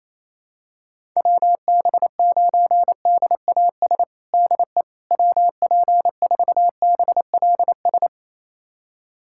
Morse code, heard as a keyed tone, about 27 wpm, 700 Hz. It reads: WB9DAS DE WP4BLS
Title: USA Callsign DE Callsign - 6 Characters - 28wpm